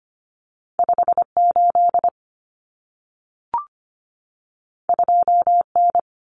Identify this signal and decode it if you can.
Morse code, heard as a keyed tone, about 25 wpm, 700 Hz. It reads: 58 2N
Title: Common Cut Numbers - 2-Digit 25wpm